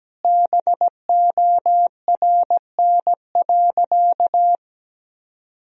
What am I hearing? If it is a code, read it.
Morse code, heard as a keyed tone, about 17 wpm, 700 Hz. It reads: BORN.
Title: Sentences from Top 1000 Words - Review Part 2 - 17wpm